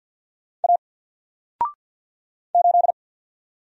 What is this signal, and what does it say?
Morse code, heard as a keyed tone, about 50 wpm, 700 Hz. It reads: A 8
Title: Single Letter-Number 50wpm